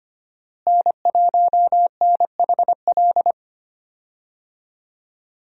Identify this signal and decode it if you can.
Morse code, heard as a keyed tone, about 25 wpm, 700 Hz. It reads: N1NHL
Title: USA Call Signs - 5 Characters - Repeated 2x - 25wpm